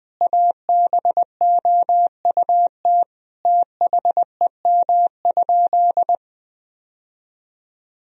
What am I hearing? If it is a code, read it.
Morse code, heard as a keyed tone, about 20 wpm, 700 Hz. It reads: ABOUT THEM?
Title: Sentences from Top 1000 Words - Review Part 1 - 20wpm